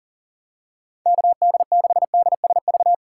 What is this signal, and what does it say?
Morse code, heard as a keyed tone, about 40 wpm, 700 Hz. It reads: KD6DSV
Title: USA Call Signs - 3x Character Spacing - 40wpm